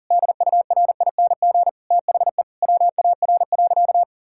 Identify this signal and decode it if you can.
Morse code, heard as a keyed tone, about 40 wpm, 700 Hz. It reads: DURING THE WAR.
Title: Sentences from Top 1000 Words - Review Part 6 - 40wpm